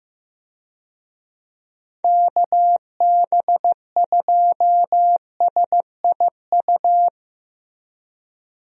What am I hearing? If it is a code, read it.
Morse code, heard as a keyed tone, about 15 wpm, 700 Hz. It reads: KB2SIU
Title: USA Call Signs - 6 Characters - Repeated 2x - 15wpm